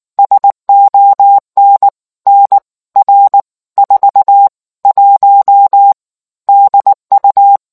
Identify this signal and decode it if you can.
Morse code, heard as a keyed tone, about 19 wpm, spaced as for 16 wpm, 800 Hz. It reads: SON NR 41 DU